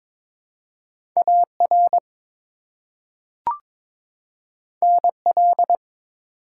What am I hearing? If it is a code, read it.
Morse code, heard as a keyed tone, about 22 wpm, 700 Hz. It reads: AR NL
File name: ARRL Sections v2 - 22wpm